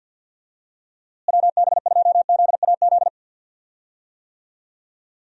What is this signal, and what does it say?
Morse code, heard as a keyed tone, about 50 wpm, 700 Hz. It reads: WB2ZAZ
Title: USA Call Signs - 6 Characters - Repeated 3x - 50wpm